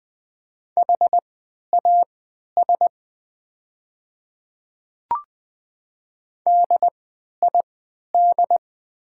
Morse code, heard as a keyed tone, about 20 wpm, spaced as for 11 wpm, 700 Hz. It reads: HAS DID
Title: 3x Character Spacing - Top 100 Words - 20wpm